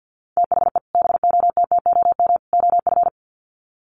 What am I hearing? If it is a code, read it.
Morse code, heard as a keyed tone, about 50 wpm, 700 Hz. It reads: THE BOTTOM OF
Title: Sets of 3 Words 50wpm